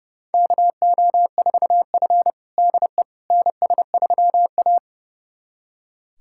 Morse code, heard as a keyed tone, about 30 wpm, 700 Hz. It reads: KO4F DE NS3A
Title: USA Callsign DE Callsign - 4 Characters - 30wpm